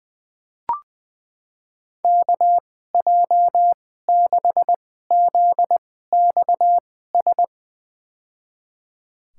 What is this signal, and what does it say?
Morse code, heard as a keyed tone, about 20 wpm, spaced as for 15 wpm, 700 Hz. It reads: KJ6ZXS
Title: USA Call Signs - 2x Character Spacing - 20wpm